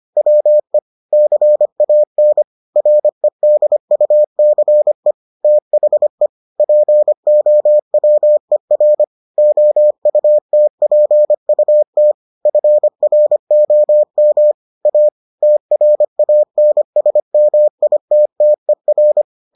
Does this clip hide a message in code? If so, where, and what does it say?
Yes: Morse code from 0.2 to 19.2 s, heard as a keyed tone, about 25 wpm, 600 Hz. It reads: WE CAN REDUCE THE POWER OUTPUT FROM A TRANSMITTER